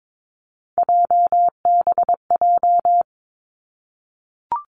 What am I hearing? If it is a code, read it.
Morse code, heard as a keyed tone, about 22 wpm, 700 Hz. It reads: JBJ
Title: Three Challenging Letter-Number Combinations 22wpm